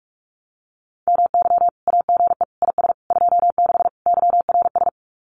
Morse code, heard as a keyed tone, about 44 wpm, 700 Hz. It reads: MY AGE IS 26 YRS
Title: Common QSO Phrases Encore 45wpm